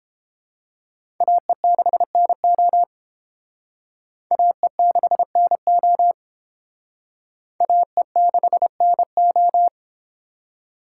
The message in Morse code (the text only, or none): AE6NO AE6NO AE6NO